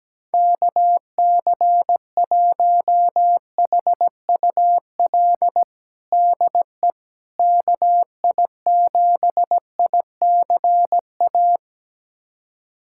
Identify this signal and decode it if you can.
Morse code, heard as a keyed tone, about 17 wpm, 700 Hz. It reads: KC1HUL DE KI7ICA
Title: USA Callsign DE Callsign - 6 Characters - 17wpm